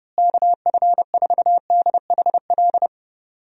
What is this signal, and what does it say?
Morse code, heard as a keyed tone, about 31 wpm, 700 Hz. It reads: KF4DHL
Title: USA Call Signs - 6 Characters - Repeated 2x - 30wpm